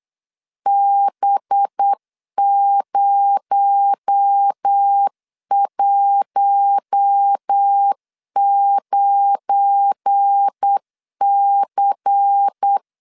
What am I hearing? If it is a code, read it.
Morse code, heard as a keyed tone, about 8 wpm, 800 Hz. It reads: B019C
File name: LEZIONI / 8 YJ45 / 8 Y 8wpm 3245